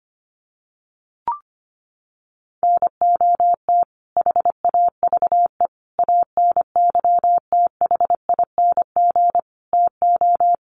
Morse code, heard as a keyed tone, about 25 wpm, 700 Hz. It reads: NOT HAVE ANYTHING TO
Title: Sets of 5 Words Encore 25wpm